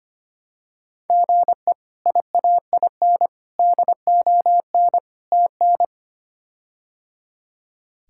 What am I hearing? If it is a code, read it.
Morse code, heard as a keyed tone, about 25 wpm, 700 Hz. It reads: GE IAIN DON TN
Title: K1USN SST QSO Practice - 25wpm